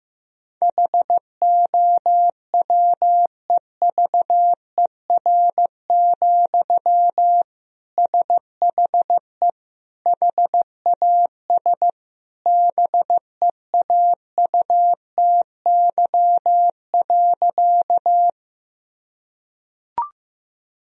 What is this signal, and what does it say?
Morse code, heard as a keyed tone, about 15 wpm, 700 Hz. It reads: HOWEVER, SHE HAS BEAUTY.
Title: Sentences from Top 1000 Words - Review Part 2 - 15wpm